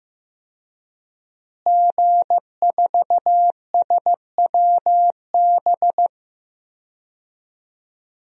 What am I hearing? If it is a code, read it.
Morse code, heard as a keyed tone, about 15 wpm, 700 Hz. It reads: G4SWB
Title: DXCC-100 Call Signs - 5 Characters - Repeated 2x - 15wpm